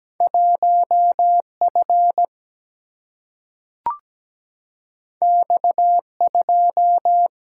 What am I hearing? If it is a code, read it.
Morse code, heard as a keyed tone, about 17 wpm, 700 Hz. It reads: 1F X2
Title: Single-Letter Single-Number 17wpm